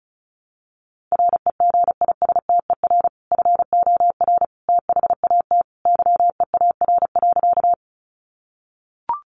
Morse code, heard as a keyed tone, about 34 wpm, 700 Hz. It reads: REGISTER FOR THAT YEAR.
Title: Sentences from Top 1200 Words 35wpm